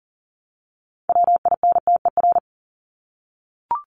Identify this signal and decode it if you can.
Morse code, heard as a keyed tone, about 40 wpm, 700 Hz. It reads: WINTER
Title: Top 1501-1750 Words 40wpm